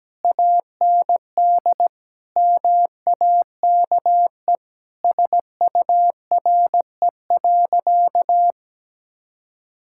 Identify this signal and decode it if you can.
Morse code, heard as a keyed tone, about 17 wpm, 700 Hz. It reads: AND MAKE SURE.
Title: Sentences from Top 1000 Words - Review Part 5 - 17wpm